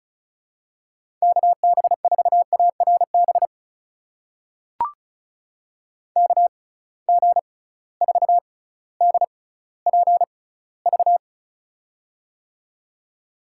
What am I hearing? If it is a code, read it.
Morse code, heard as a keyed tone, about 34 wpm, spaced as for 18 wpm, 700 Hz. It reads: KB4ARB KG4DPV
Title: USA Call Signs - 6 Characters - 6x Character Spacing - 35wpm